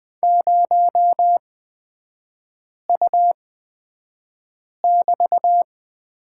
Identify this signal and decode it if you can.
Morse code, heard as a keyed tone, about 20 wpm, spaced as for 5 wpm, 700 Hz. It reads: 0U<BT>
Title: 224 Lesson 36e - BT - Three Characters 20wpm